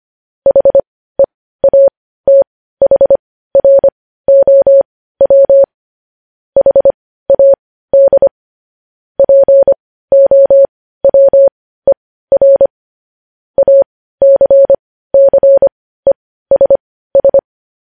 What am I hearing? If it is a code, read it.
Morse code, heard as a keyed tone, about 25 wpm, spaced as for 15 wpm, 550 Hz. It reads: HEATHROW HAD POWER ACCESS
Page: news-headlines-in-morse-code-at-15-wpm